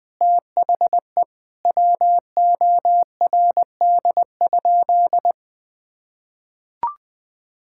Morse code, heard as a keyed tone, about 20 wpm, 700 Hz. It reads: THE WORD?
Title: Sentences from Top 1000 Words - Review Part 6 - 20wpm